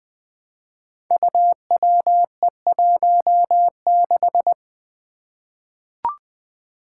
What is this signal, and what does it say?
Morse code, heard as a keyed tone, about 20 wpm, 700 Hz. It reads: UWE16
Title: Five Challenging Letter-Number Combinations 20wpm